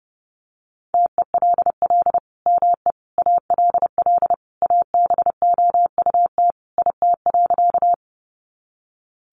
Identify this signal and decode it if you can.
Morse code, heard as a keyed tone, about 30 wpm, 700 Hz. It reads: TELL ME ALL ABOUT IT.
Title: Sentences from Top 200 Words 30wpm